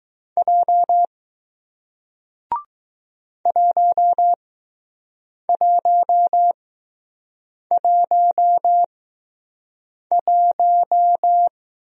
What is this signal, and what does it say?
Morse code, heard as a keyed tone, about 20 wpm, spaced as for 6 wpm, 700 Hz. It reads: J 1111